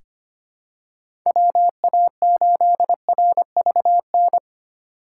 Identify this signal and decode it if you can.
Morse code, heard as a keyed tone, about 25 wpm, 700 Hz. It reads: WA8RVN
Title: USA Call Signs - 6 Characters - 3x Character Spacing - 25wpm